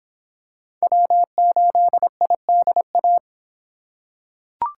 Morse code, heard as a keyed tone, about 25 wpm, 700 Hz. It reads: W8IDA